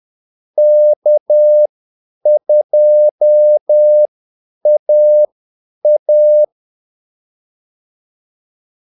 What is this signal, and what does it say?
Morse code, heard as a keyed tone, about 10 wpm, spaced as for 8 wpm, 600 Hz. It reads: K2AA
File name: Callsigns-10-8wpm